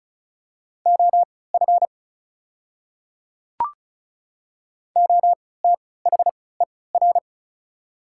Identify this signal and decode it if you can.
Morse code, heard as a keyed tone, about 34 wpm, spaced as for 20 wpm, 700 Hz. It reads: OF OTHER
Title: 3x Character Spacing - Top 100 Words - 35wpm